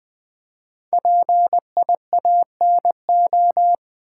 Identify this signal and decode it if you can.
Morse code, heard as a keyed tone, about 20 wpm, 700 Hz. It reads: PIANO
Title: Five Letter Words - 20wpm